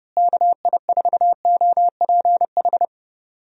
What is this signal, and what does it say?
Morse code, heard as a keyed tone, about 32 wpm, 700 Hz. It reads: KI4OPH
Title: USA Call Signs - 6 Characters - Repeated 2x - 30wpm